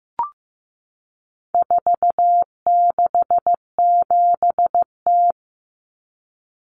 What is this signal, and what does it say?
Morse code, heard as a keyed tone, about 15 wpm, 700 Hz. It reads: 467T
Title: Common Cut Numbers - 4-Digit 15wpm